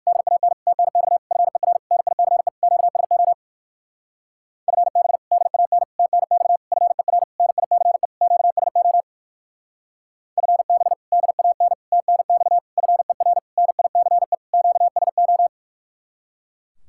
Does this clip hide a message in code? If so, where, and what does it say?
Yes: Morse code from 0.1 to 15.5 s, heard as a keyed tone, about 62 wpm, 700 Hz. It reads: DAN TNX FER NICE QSO FB DAN TNX FER NICE QSO FB DAN TNX FER NICE QSO